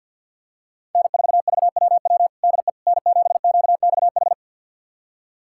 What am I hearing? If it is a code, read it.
Morse code, heard as a keyed tone, about 51 wpm, 690 Hz. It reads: N4VWW DE N7QXH
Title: USA Callsign DE Callsign - 5 Characters - 50wpm